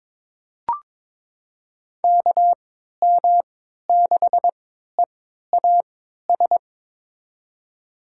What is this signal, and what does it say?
Morse code, heard as a keyed tone, about 22 wpm, spaced as for 13 wpm, 700 Hz. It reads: KM6EAS
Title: USA Call Signs - 3x Character Spacing - 22wpm